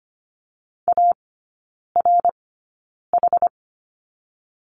Morse code, heard as a keyed tone, about 25 wpm, spaced as for 9 wpm, 700 Hz. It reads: ARH
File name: Lesson 14e - Number 2 - Three Characters 25wpm